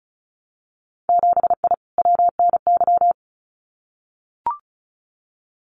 Morse code, heard as a keyed tone, about 34 wpm, 700 Hz. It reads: 7I WNY
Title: Winter Field Day Exchanges Encore 35wpm